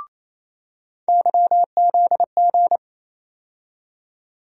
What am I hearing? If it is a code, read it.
Morse code, heard as a keyed tone, about 28 wpm, 700 Hz. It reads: YZG
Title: USA Call Sign Postfixes - 3 Letters 28wpm